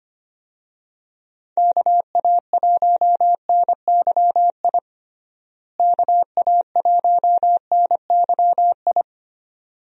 Morse code, heard as a keyed tone, about 25 wpm, 700 Hz. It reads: KA1NYI KA1NYI
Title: USA Call Signs - 6 Characters - Repeated 2x - 25wpm